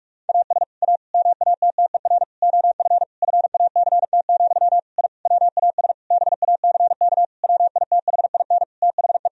Morse code, heard as a keyed tone, about 46 wpm, 700 Hz. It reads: AS A MATTER OF FACT, I WAS BACK WITHIN THE
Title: Sentences from Top 1000 Words - Review Part 4 - 45wpm